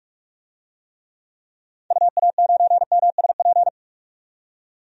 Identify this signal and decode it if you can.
Morse code, heard as a keyed tone, about 46 wpm, 700 Hz. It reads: UA9MSP
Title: DXCC-100 Call Signs - 6 Characters - 45wpm